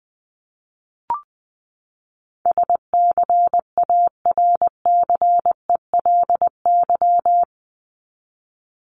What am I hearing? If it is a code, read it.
Morse code, heard as a keyed tone, about 20 wpm, 700 Hz. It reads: SCARCELY
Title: LY Suffix - Top 100 Words with Base Words 20wpm